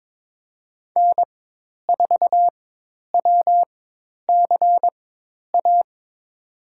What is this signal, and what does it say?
Morse code, heard as a keyed tone, about 21 wpm, spaced as for 10 wpm, 700 Hz. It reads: N4WCA
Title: USA Call Signs - 5 Characters - 4x Character Spacing - 22wpm